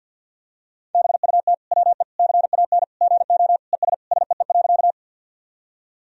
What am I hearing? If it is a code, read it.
Morse code, heard as a keyed tone, about 52 wpm, 700 Hz. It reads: BUT WE CAN GO ES SEE.
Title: Sentences from Top 100 Words and Abbreviations 50wpm